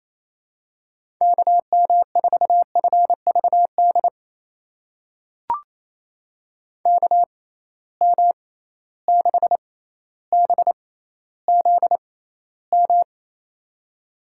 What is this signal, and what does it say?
Morse code, heard as a keyed tone, about 28 wpm, spaced as for 14 wpm, 700 Hz. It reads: KM4FVD KM6BZM
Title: USA Call Signs - 6 Characters - 6x Character Spacing - 28wpm